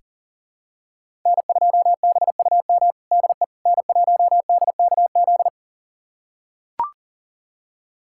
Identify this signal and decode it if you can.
Morse code, heard as a keyed tone, about 40 wpm, 700 Hz. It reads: N2BUM DE N1DKZ
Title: USA Callsign DE Callsign - 5 Characters - 40wpm